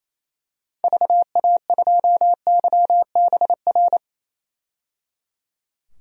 Morse code, heard as a keyed tone, about 28 wpm, 700 Hz. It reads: VA2YBR
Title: DXCC-100 Call Signs - 6 Characters - 28wpm